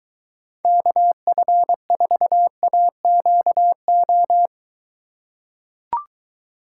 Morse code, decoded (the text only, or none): KF4AQO